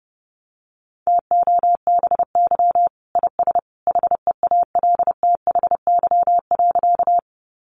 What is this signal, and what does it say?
Morse code, heard as a keyed tone, about 30 wpm, 700 Hz. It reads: TOBY IS HEALTHY.
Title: Easy Phrases 30wpm